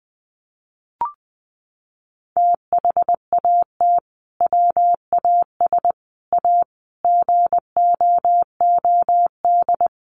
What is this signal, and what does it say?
Morse code, heard as a keyed tone, about 20 wpm, 700 Hz. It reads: THAT WAS A GOOD
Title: Sentences from Top 900 Words 20wpm